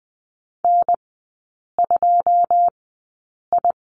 Morse code, heard as a keyed tone, about 20 wpm, spaced as for 8 wpm, 690 Hz. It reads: N2I
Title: 087 Lesson 15e - Number 5 - Three Characters 20wpm